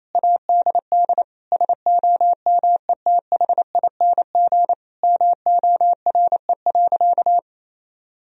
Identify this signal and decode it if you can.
Morse code, heard as a keyed tone, about 28 wpm, 700 Hz. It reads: ADD SOMETHING MORE.
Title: Sentences from Top 1000 Words - Review Part 1 - 28wpm